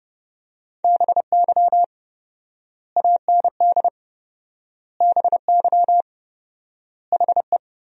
Morse code, heard as a keyed tone, about 30 wpm, 700 Hz. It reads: BY AND BY HE
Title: 4x Word-Spacing - Sentences from Top 300 Words - 30wpm